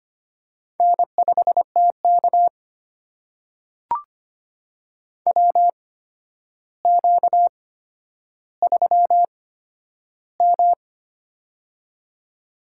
Morse code, heard as a keyed tone, about 25 wpm, spaced as for 11 wpm, 700 Hz. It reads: N5TK WQ3M
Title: USA Call Signs - 4 Characters - 8x Character Spacing - 25wpm